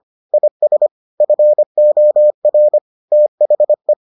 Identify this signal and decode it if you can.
Morse code, heard as a keyed tone, about 25 wpm, 600 Hz. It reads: IS FOR THE